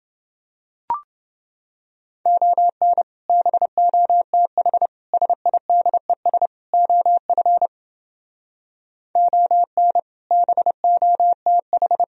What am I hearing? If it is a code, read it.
Morse code, heard as a keyed tone, about 28 wpm, 700 Hz. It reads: ON BOTH SIDES OF ON BOTH